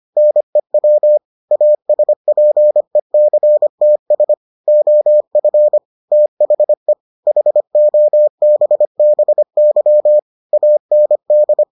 Morse code, heard as a keyed tone, about 25 wpm, 600 Hz. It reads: NEW ASPECTS OF THE HOBBY AND